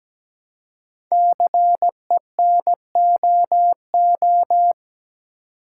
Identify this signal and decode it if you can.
Morse code, heard as a keyed tone, about 17 wpm, 700 Hz. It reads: CENOO
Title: Code Groups - 5 Letters - 17wpm